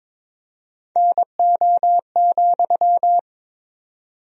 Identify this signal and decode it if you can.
Morse code, heard as a keyed tone, about 22 wpm, 700 Hz. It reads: NO,
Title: 4x Word-Spacing - Sentences from Top 300 Words - 22wpm